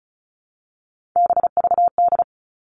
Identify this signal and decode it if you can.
Morse code, heard as a keyed tone, about 36 wpm, 700 Hz. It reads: BVD
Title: Three Challenging Letter-Number Combinations 35wpm